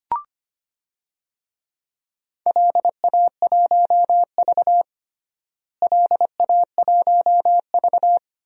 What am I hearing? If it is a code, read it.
Morse code, heard as a keyed tone, about 25 wpm, 700 Hz. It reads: LA1V LA1V
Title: DXCC-100 Call Signs - 4 Characters - Repeated 2x - 25wpm